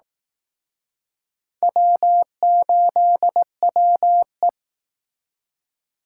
Morse code, heard as a keyed tone, about 20 wpm, 700 Hz. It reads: W8WE